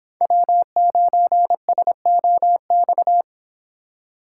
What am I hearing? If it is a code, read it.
Morse code, heard as a keyed tone, about 26 wpm, 700 Hz. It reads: W9SOX